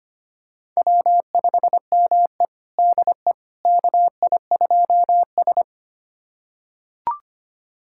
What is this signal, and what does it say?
Morse code, heard as a keyed tone, about 25 wpm, 700 Hz. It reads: W5ME DE KI2S
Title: USA Callsign DE Callsign - 4 Characters - 25wpm